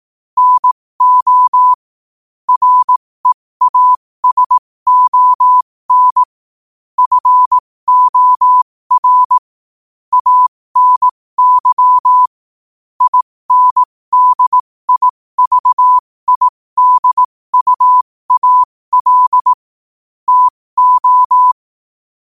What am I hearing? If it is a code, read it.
Morse code, heard as a keyed tone, about 18 wpm, spaced as for 15 wpm, 1000 Hz. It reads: NO REASON FOR ANY INDIVIDUAL TO